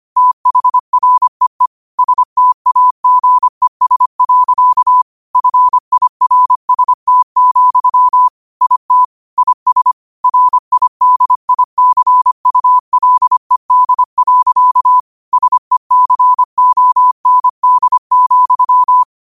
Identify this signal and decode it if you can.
Morse code, heard as a keyed tone, about 25 wpm, 1000 Hz. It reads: THREE STAGES. FIRST, IT IS RIDICULED. SECOND,